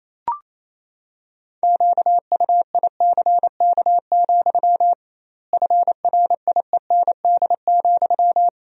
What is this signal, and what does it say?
Morse code, heard as a keyed tone, about 27 wpm, 700 Hz. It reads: QUICK, FRIEND,
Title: 2x Word-Spacing - Sentences from Top 500 Words - 28wpm